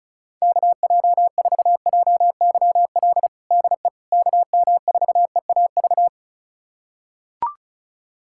Morse code, heard as a keyed tone, about 36 wpm, 700 Hz. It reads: KJ4JYL DE KM4EAV
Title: USA Callsign DE Callsign 35wpm